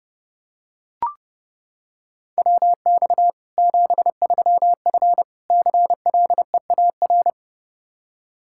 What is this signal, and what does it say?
Morse code, heard as a keyed tone, about 30 wpm, 700 Hz. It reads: WX 73F CLEAR
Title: Practice QSO - Exchange 2 - Info - 30wpm